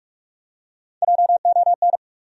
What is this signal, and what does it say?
Morse code, heard as a keyed tone, about 47 wpm, 700 Hz. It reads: JON
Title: USA Common Names - Max 4 Letters - 45wpm